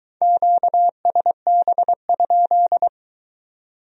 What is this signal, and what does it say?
Morse code, heard as a keyed tone, about 23 wpm, 700 Hz. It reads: QSB?